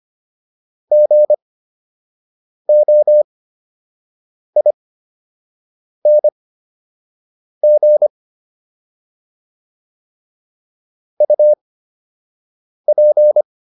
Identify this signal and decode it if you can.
Morse code, heard as a keyed tone, about 25 wpm, spaced as for 6 wpm, 600 Hz. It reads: GOING UP